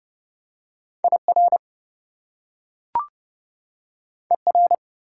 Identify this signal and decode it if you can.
Morse code, heard as a keyed tone, about 30 wpm, 700 Hz. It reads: IR ER
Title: Prefixes and Suffixes 30wpm